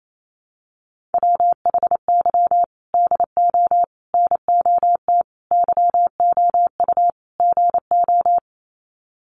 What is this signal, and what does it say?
Morse code, heard as a keyed tone, about 28 wpm, 700 Hz. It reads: WHY DO NOT YOU GO
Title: Sets of 5 Words Encore 28wpm